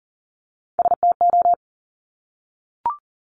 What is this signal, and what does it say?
Morse code, heard as a keyed tone, about 40 wpm, 700 Hz. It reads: STO
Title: Top Three Letter Combinations in Words 40wpm